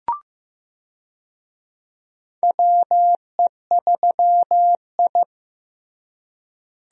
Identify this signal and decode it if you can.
Morse code, heard as a keyed tone, about 16 wpm, 700 Hz. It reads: WE3I
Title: USA Call Signs - 4 Characters - Repeated 3x - 15wpm